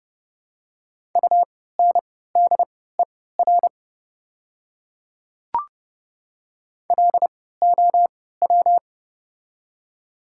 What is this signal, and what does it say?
Morse code, heard as a keyed tone, about 30 wpm, spaced as for 17 wpm, 700 Hz. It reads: UNDER LOW
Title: 3x Character Spacing - Top 200 Words - 30wpm